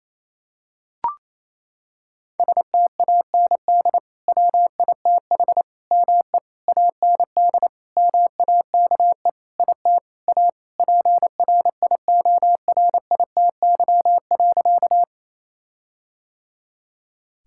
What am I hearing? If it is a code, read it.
Morse code, heard as a keyed tone, about 28 wpm, 700 Hz. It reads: STAND WITH ME AND MAKE IT A PRIORITY.
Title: Sentences from Top 1600 Words 28wpm